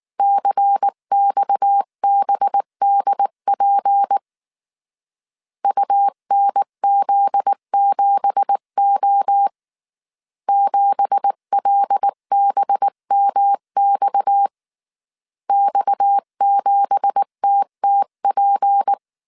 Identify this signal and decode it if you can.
Morse code, heard as a keyed tone, about 19 wpm, 800 Hz. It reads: CXBDP UNZ7O 7LBMX X7TTP
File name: CW / DET / 1 9 1500C S B 18wpm 2551